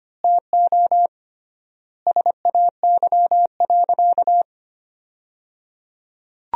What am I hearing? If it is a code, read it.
Morse code, heard as a keyed tone, about 25 wpm, 700 Hz. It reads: TO SAY.
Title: 3x Word-Spacing - Sentences from Top 500 Words - 25wpm